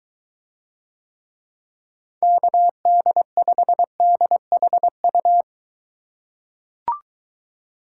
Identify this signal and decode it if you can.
Morse code, heard as a keyed tone, about 24 wpm, 700 Hz. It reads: KD5DHU